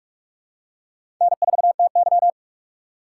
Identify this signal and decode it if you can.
Morse code, heard as a keyed tone, about 46 wpm, 700 Hz. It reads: N4TY
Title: USA Call Signs - 3x Character Spacing - 45wpm